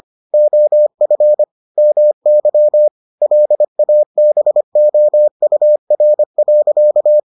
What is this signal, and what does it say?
Morse code, heard as a keyed tone, about 25 wpm, 600 Hz. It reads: OF MY LABOUR.